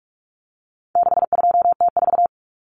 Morse code, heard as a keyed tone, about 44 wpm, 700 Hz. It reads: 62T4
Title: Common Cut Numbers - 4-Digit 45wpm